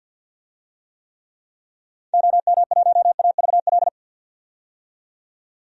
Morse code, heard as a keyed tone, about 52 wpm, 700 Hz. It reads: OM1AVL
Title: DXCC-100 Call Signs - 6 Characters - 50wpm